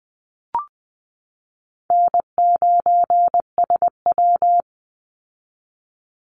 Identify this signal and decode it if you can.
Morse code, heard as a keyed tone, about 20 wpm, 700 Hz. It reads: N9SW
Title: Easy Call Signs Repeated 3x - 20wpm